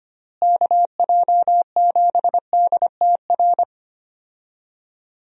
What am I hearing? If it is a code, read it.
Morse code, heard as a keyed tone, about 25 wpm, 700 Hz. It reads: KJ7DTR
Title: USA Call Signs - 6 Characters - Repeated 3x - 25wpm